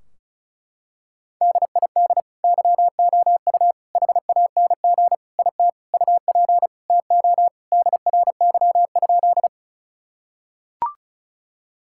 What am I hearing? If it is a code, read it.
Morse code, heard as a keyed tone, about 36 wpm, 700 Hz. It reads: DID YOU HANG IT UP TO DRY?
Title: Sentences from Top 1600 Words 35wpm